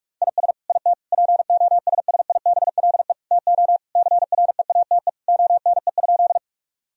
Morse code, heard as a keyed tone, about 46 wpm, 700 Hz. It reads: IS IT POSSIBLE TO CREATE ONE?
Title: Sentences from Top 1000 Words - Review Part 3 - 45wpm